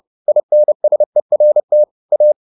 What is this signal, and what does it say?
Morse code, heard as a keyed tone, about 30 wpm, 600 Hz. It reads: INSERT A